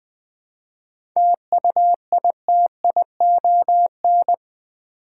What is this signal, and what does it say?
Morse code, heard as a keyed tone, about 20 wpm, 700 Hz. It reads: TUITION
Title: 7 Letter Words - First 3 Letters Given - 20wpm